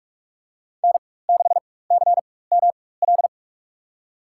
Morse code, heard as a keyed tone, about 44 wpm, spaced as for 21 wpm, 700 Hz. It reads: N6CML
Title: USA Call Signs - 5 Characters - 4x Character Spacing - 45wpm